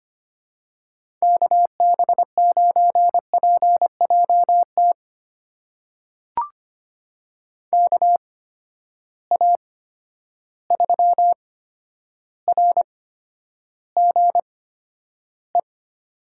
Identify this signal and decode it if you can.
Morse code, heard as a keyed tone, about 25 wpm, spaced as for 11 wpm, 700 Hz. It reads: KB9PJT KA3RGE
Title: USA Call Signs - 6 Characters - 8x Character Spacing - 25wpm